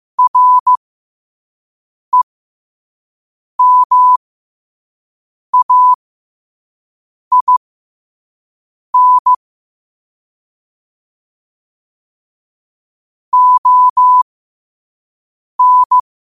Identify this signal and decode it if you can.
Morse code, heard as a keyed tone, about 15 wpm, spaced as for 5 wpm, 1000 Hz. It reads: REMAIN ON
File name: qotdfiles / 5wpm